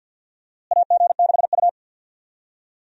Morse code, heard as a keyed tone, about 46 wpm, 700 Hz. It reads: AG6U
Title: USA Call Signs - 4 Characters - Repeated 3x - 50wpm